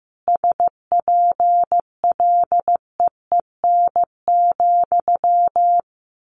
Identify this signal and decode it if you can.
Morse code, heard as a keyed tone, about 15 wpm, 700 Hz. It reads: SPLEEN,
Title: A Sound Of Thunder 15wpm